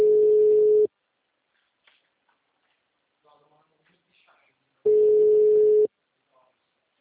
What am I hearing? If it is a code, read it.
Ringback tone (Europe, 425 Hz), with speech.